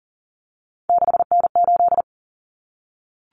Morse code, heard as a keyed tone, about 40 wpm, 700 Hz. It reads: 6N8
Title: Common Cut Numbers - 3-Digit 40wpm